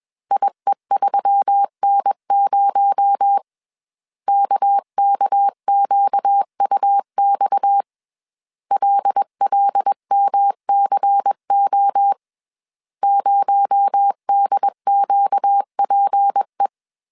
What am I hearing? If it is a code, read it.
Morse code, heard as a keyed tone, about 21 wpm, 800 Hz. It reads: IE3N0 KKQUX LLMCO 0DQPE
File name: CW / DET / 1 6 1500C S C 20wpm 2116